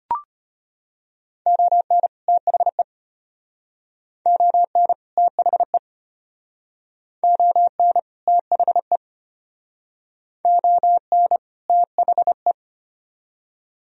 Morse code, heard as a keyed tone, about 32 wpm, 700 Hz. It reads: ON THE ON THE ON THE ON THE